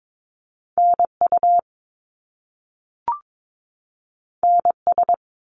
Morse code, heard as a keyed tone, about 22 wpm, 700 Hz. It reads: NU NS
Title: Canadian Provinces and Territories 22wpm